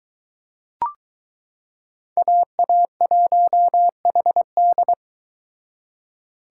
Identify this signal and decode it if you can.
Morse code, heard as a keyed tone, about 22 wpm, 700 Hz. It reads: AA1HD